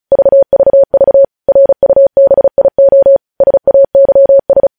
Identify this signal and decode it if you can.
Morse code, heard as a keyed tone, about 36 wpm, 550 Hz. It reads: VVV RUBIO SAYS
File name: podcast / News headlines-1x-35x35